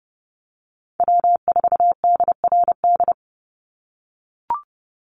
Morse code, heard as a keyed tone, about 30 wpm, 700 Hz. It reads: W4DRD
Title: USA Call Signs 30wpm